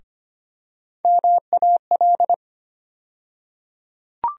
Morse code, heard as a keyed tone, about 25 wpm, 700 Hz. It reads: MAL
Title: USA Call Sign Postfixes - 3 Letters 25wpm